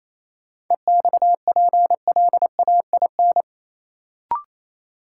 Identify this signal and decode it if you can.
Morse code, heard as a keyed tone, about 28 wpm, 700 Hz. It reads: EXPLAIN
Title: Sentences from Top 1100 Words - New Word Drill - 28wpm